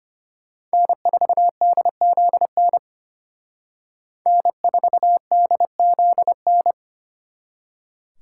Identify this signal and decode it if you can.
Morse code, heard as a keyed tone, about 29 wpm, 700 Hz. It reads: N4DZN N4DZN
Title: Speed Racing - USA 1x3 Call Signs 25wpm